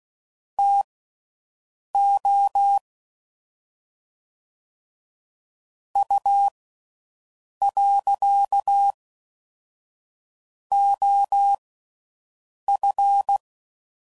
Morse code, heard as a keyed tone, about 16 wpm, spaced as for 6 wpm, 800 Hz. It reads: TO U.OF
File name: CW Code Groups @5WPM / QSOs / QSO WA7QOX